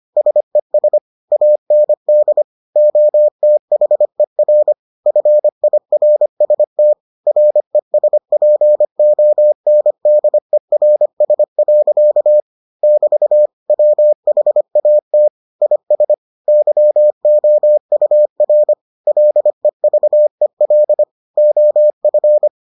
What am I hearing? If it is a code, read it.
Morse code, heard as a keyed tone, about 25 wpm, 600 Hz. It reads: SES AND OTHER FIRST RESPONDERS. <BT> WHAT IS YOUR LEVEL OF